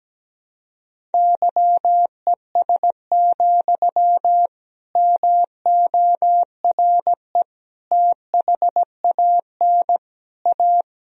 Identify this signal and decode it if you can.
Morse code, heard as a keyed tone, about 17 wpm, 700 Hz. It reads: YES, MORE THAN A
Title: Sentences from Top 1000 Words - Review Part 1 - 17wpm